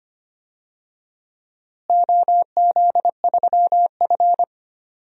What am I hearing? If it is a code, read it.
Morse code, heard as a keyed tone, about 25 wpm, 700 Hz. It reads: OZ3F
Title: DXCC-100 Call Signs - 4 Characters - Repeated 3x - 25wpm